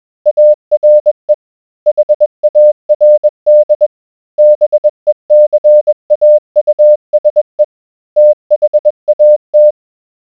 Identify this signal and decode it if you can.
Morse code, heard as a keyed tone, about 21 wpm, 600 Hz. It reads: ARE HARD BECAUSE THAT